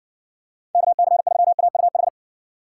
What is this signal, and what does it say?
Morse code, heard as a keyed tone, about 56 wpm, 700 Hz. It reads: KC3AUH